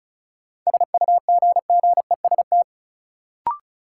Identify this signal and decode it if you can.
Morse code, heard as a keyed tone, about 34 wpm, 700 Hz. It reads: SUGGEST
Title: Sentences from Top 1300 Words - New Word Drill Alt - 35wpm